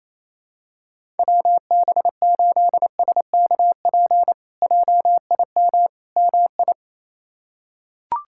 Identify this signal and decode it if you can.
Morse code, heard as a keyed tone, about 28 wpm, 700 Hz. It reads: WB8SKP JIM MI
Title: K1USN SST QSO Practice - 28wpm